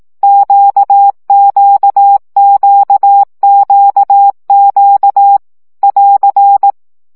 Morse code, heard as a keyed tone, about 18 wpm, 800 Hz. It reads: QQQQQ <AR>
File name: CW gyakorlas q 15 baud